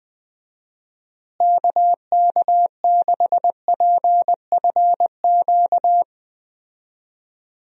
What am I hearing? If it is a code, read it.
Morse code, heard as a keyed tone, about 21 wpm, 700 Hz. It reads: KK6PFQ
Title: USA Call Signs - 6 Characters - Repeated 3x - 20wpm